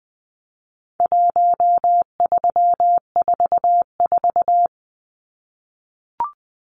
Morse code, heard as a keyed tone, about 20 wpm, 700 Hz. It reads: 1344
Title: Numbers - 4-Digit 20wpm